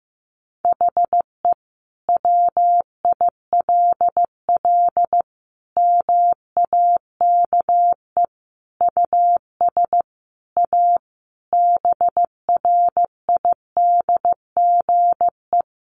Morse code, heard as a keyed tone, about 15 wpm, 700 Hz. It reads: HE WILL MAKE US A BRIDGE
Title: Sentences from Top 1200 Words 15wpm